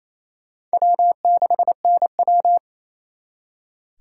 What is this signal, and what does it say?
Morse code, heard as a keyed tone, about 36 wpm, 700 Hz. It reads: W6NW